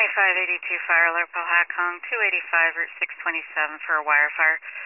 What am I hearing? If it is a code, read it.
Emergency Alert System (SAME) tones, heard as harsh buzzing bursts, with speech. It is heard from 0.8 to 4.7 s.